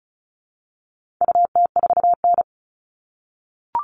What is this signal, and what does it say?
Morse code, heard as a keyed tone, about 34 wpm, 700 Hz. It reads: UT4N
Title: All Cut Numbers - 4-Digit 35wpm